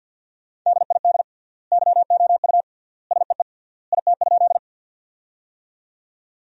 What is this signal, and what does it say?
Morse code, heard as a keyed tone, about 53 wpm, 700 Hz. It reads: DID YOU SEE IT?
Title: 3x Word-Spacing - Sentences from Top 100 Words - 50wpm